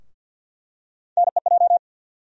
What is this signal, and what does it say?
Morse code, heard as a keyed tone, about 50 wpm, 700 Hz. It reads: NEJ
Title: USA Call Sign Postfixes - 3 Letters 50wpm